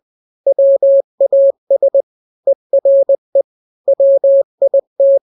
Morse code, heard as a keyed tone, about 20 wpm, 550 Hz. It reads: WAS ERE WIT
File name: n-grams / 3-grams 20 18 WPM